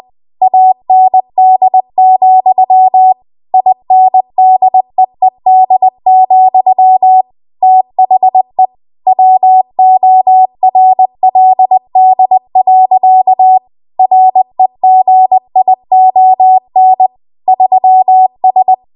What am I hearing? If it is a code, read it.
Morse code, heard as a keyed tone, about 20 wpm, 750 Hz. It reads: AND, INDEED, THE WORLD. REGION 3S
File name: Archive / 20 WPM / 20WPM